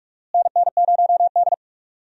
Morse code, heard as a keyed tone, about 46 wpm, 700 Hz. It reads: NN0D
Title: USA Call Signs - 4 Characters - Repeated 2x - 45wpm